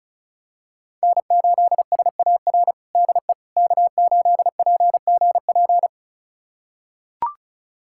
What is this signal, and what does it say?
Morse code, heard as a keyed tone, about 36 wpm, 700 Hz. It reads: N8SAR DE K8PGP
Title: USA Callsign DE Callsign - 5 Characters - 35wpm